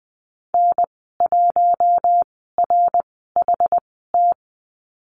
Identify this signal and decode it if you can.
Morse code, heard as a keyed tone, about 20 wpm, spaced as for 15 wpm, 700 Hz. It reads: N1RHT
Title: 082 Lesson 14g - Number 2 - Call Signs 20wpm